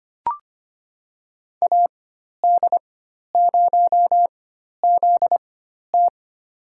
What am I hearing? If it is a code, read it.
Morse code, heard as a keyed tone, about 25 wpm, spaced as for 12 wpm, 700 Hz. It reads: AD0ZT
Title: USA Call Signs - 4x Character Spacing - 25wpm